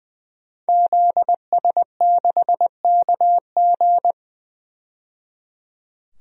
Morse code, heard as a keyed tone, about 21 wpm, 700 Hz. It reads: ZS6KG
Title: DXCC-100 Call Signs - 5 Characters - 20wpm